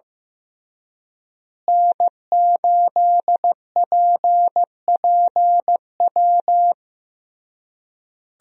Morse code, heard as a keyed tone, about 16 wpm, 700 Hz. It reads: N8PPW
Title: USA Call Signs - 5 Characters - Repeated 3x - 15wpm